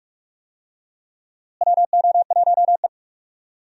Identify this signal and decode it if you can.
Morse code, heard as a keyed tone, about 46 wpm, 700 Hz. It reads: WO1E
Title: USA Call Signs - 4 Characters - Repeated 2x - 45wpm